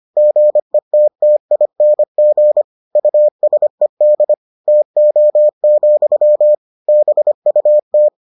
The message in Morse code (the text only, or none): GETTING USED TO, BUT